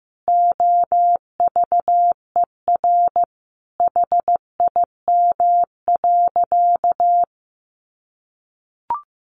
Morse code, heard as a keyed tone, about 15 wpm, 700 Hz. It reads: OVER HIM.
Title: Sentences from Top 1200 Words 15wpm